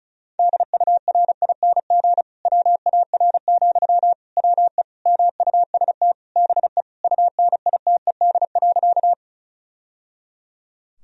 Morse code, heard as a keyed tone, about 34 wpm, 700 Hz. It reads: DURING WAR, WE MUST BE UNITED.
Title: Sentences from Top 1000 Words - Review Part 4 - 35wpm